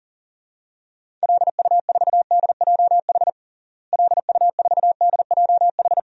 Morse code, heard as a keyed tone, about 37 wpm, 700 Hz. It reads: LU4DJH LU4DJH
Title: DXCC-100 Call Signs - 6 Characters - Repeated 2x - 40wpm